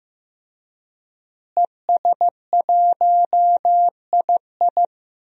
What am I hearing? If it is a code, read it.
Morse code, heard as a keyed tone, about 15 wpm, 700 Hz. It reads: ES1II
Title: DXCC-100 Call Signs - 5 Characters - Repeated 3x - 15wpm